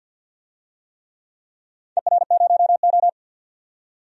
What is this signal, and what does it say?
Morse code, heard as a keyed tone, about 51 wpm, 700 Hz. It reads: ER0O
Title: DXCC-100 Call Signs - 4 Characters - 50wpm